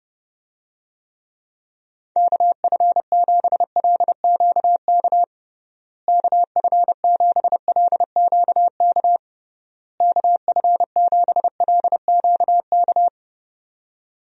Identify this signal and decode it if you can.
Morse code, heard as a keyed tone, about 30 wpm, 700 Hz. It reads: KF7LQK KF7LQK KF7LQK
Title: USA Call Signs - 6 Characters - Repeated 3x - 30wpm